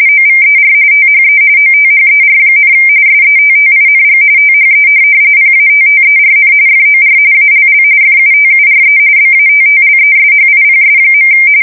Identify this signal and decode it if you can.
RTTY, about 45.45 baud. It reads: OVER THE OTHER, SETTING THE PATTERN FOR ALL FUTURE BATTLESHIPS.
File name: fsk170 / USS-New-Jersey